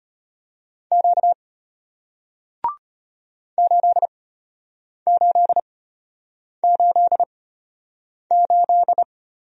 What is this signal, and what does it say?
Morse code, heard as a keyed tone, about 33 wpm, spaced as for 8 wpm, 700 Hz. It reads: Q 8888